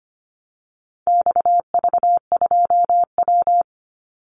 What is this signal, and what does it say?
Morse code, heard as a keyed tone, about 25 wpm, 700 Hz. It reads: XV2W
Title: Call Signs Repeated 3x - 25wpm